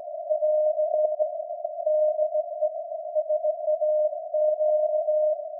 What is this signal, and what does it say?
Morse code, heard as a keyed tone, about 17 wpm, 610 Hz. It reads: <AR> DE SAQ
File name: iz7slz / VLF / SAQ jun 2019